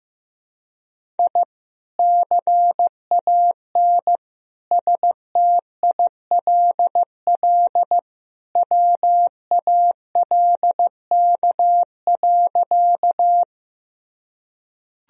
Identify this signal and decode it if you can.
Morse code, heard as a keyed tone, about 15 wpm, 700 Hz. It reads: I CAN STILL WALK.
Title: Sentences from Top 1000 Words - Review Part 1 - 15wpm